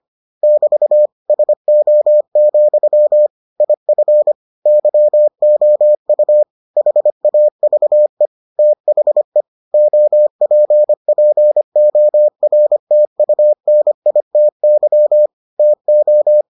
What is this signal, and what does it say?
Morse code, heard as a keyed tone, about 25 wpm, 600 Hz. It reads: <BT> SO, IF YOU HAVE THE OPPORTUNITY TO